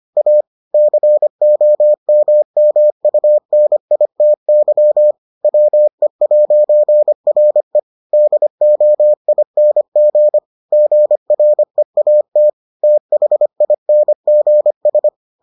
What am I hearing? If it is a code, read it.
Morse code, heard as a keyed tone, about 25 wpm, 600 Hz. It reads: A COMMUNITY WE'RE DOING GREAT THINGS